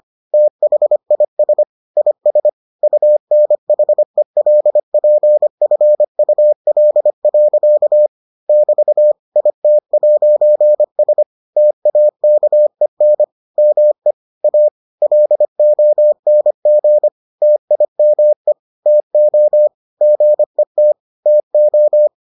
Morse code, heard as a keyed tone, about 25 wpm, 600 Hz. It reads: THIS IS UNHELPFUL. <BT> IT'S TAKEN ME A LONG TIME TO GET TO